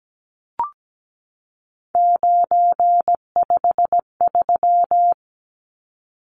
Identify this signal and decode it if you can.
Morse code, heard as a keyed tone, about 17 wpm, 700 Hz. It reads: 953
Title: Numbers - 3-Digit 17wpm